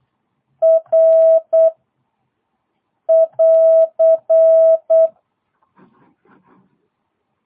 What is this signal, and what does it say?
Morse code, heard as a keyed tone, about 8 wpm, 650 Hz. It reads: R <AR>